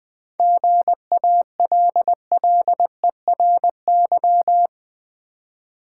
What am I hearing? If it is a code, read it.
Morse code, heard as a keyed tone, about 20 wpm, 700 Hz. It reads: GALLERY
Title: 7 Letter Words - First 3 Letters Given - 20wpm